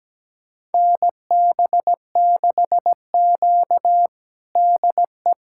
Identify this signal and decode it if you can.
Morse code, heard as a keyed tone, about 17 wpm, 700 Hz. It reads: NB6Q DE
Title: USA Callsign DE Callsign - 4 Characters - 17wpm